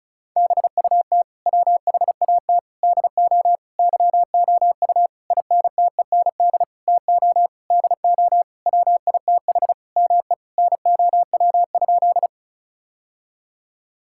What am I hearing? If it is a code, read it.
Morse code, heard as a keyed tone, about 36 wpm, 700 Hz. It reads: BUT WHAT DO YOU INTEND TO DO WITH ME NOW?
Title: Sentences from Top 1600 Words 35wpm